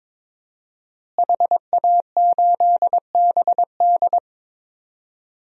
Morse code, heard as a keyed tone, about 22 wpm, 700 Hz. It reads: HA8BD
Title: DXCC-100 Call Signs - 5 Characters - Repeated 2x - 22wpm